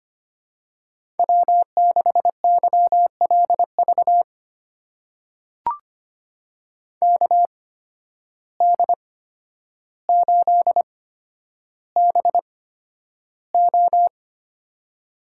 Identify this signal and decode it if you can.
Morse code, heard as a keyed tone, about 25 wpm, spaced as for 11 wpm, 700 Hz. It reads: W6YLV KD8BO
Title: USA Call Signs - 8x Character Spacing - 25wpm